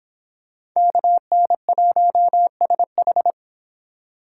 Morse code, heard as a keyed tone, about 26 wpm, 700 Hz. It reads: KN1SH